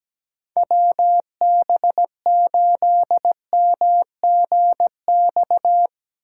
Morse code, heard as a keyed tone, about 17 wpm, 700 Hz. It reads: WB8MGX
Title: USA Call Signs - 6 Characters - 4x Character Spacing - 17wpm